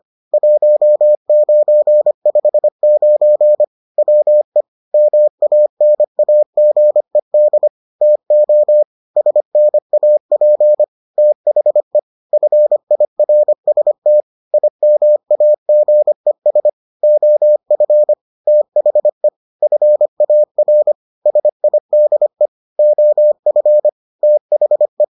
Morse code, heard as a keyed tone, about 25 wpm, 600 Hz. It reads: 1959 WE MANAGED TO SNAP THE FIRST IMAGES OF THE FAR SIDE OF THE